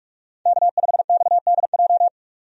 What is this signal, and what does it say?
Morse code, heard as a keyed tone, about 46 wpm, 700 Hz. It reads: K5XDJ
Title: USA Call Signs - 5 Characters - 45wpm